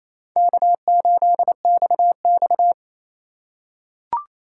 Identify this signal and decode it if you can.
Morse code, heard as a keyed tone, about 29 wpm, 700 Hz. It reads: K8XX
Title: USA Call Signs - 2x Character Spacing - 28wpm